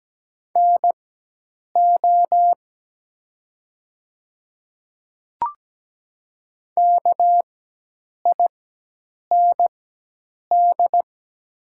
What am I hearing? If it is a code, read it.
Morse code, heard as a keyed tone, about 17 wpm, spaced as for 8 wpm, 700 Hz. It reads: NO KIND
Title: 4x Character Spacing - Top 200 Words - 17wpm